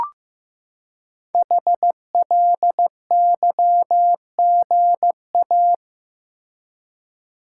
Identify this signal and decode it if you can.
Morse code, heard as a keyed tone, about 15 wpm, 700 Hz. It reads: HLYGA
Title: Code Groups - 5 Letters - 15wpm